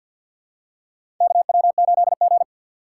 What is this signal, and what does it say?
Morse code, heard as a keyed tone, about 53 wpm, 700 Hz. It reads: KW8G
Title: USA Call Signs - 4x Character Spacing - 50wpm